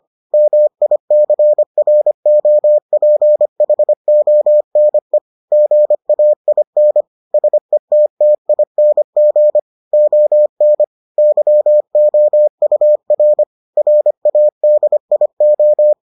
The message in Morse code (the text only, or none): MICROPHONE GAIN SETTING ON YOUR RADIO